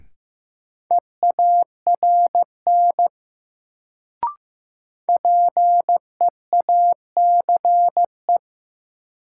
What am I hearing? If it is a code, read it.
Morse code, heard as a keyed tone, about 15 wpm, 700 Hz. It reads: EARN PEACE
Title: Sentences from Top 1500 Words - New Word Drill - 15wpm